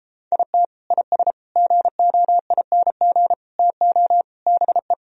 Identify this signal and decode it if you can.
Morse code, heard as a keyed tone, about 33 wpm, 700 Hz. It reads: IT IS GOING TO BE